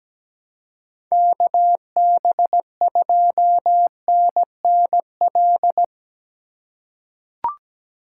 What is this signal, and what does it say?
Morse code, heard as a keyed tone, about 17 wpm, 700 Hz. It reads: KB2NNL
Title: USA Call Signs - 6 Characters - 2x Character Spacing - 17wpm